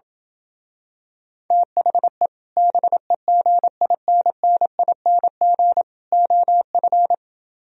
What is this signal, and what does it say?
Morse code, heard as a keyed tone, about 28 wpm, 700 Hz. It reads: THE BEGINNING OF